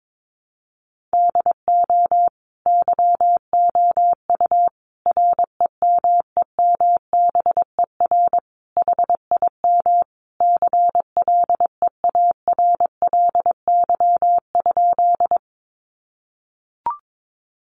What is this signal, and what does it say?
Morse code, heard as a keyed tone, about 22 wpm, 700 Hz. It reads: DO YOU REMEMBER HIM CLEARLY?
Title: Sentences from Top 1300 Words 22wpm